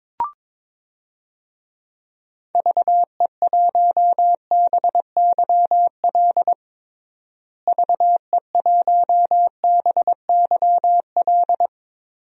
Morse code, heard as a keyed tone, about 22 wpm, 700 Hz. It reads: VE1BYL VE1BYL
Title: DXCC-100 Call Signs - 6 Characters - Repeated 3x - 22wpm